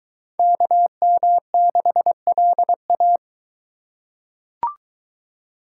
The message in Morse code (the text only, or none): KM6LA